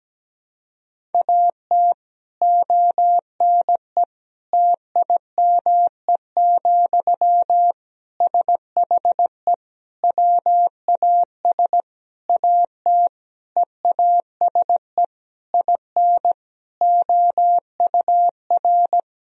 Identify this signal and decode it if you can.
Morse code, heard as a keyed tone, about 17 wpm, 700 Hz. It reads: AT ONE TIME, SHE WAS AT EASE IN OUR
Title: Sentences from Top 1000 Words - Review Part 1 - 17wpm